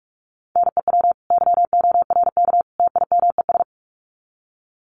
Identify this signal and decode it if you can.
Morse code, heard as a keyed tone, about 44 wpm, 700 Hz. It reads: NEW YORK TIMES
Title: Sets of 3 Words 45wpm